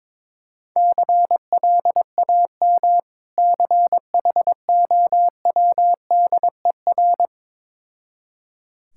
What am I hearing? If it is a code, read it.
Morse code, heard as a keyed tone, about 22 wpm, 690 Hz. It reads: CLAM CHOWDER
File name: Difficult Words with Hints - 22wpm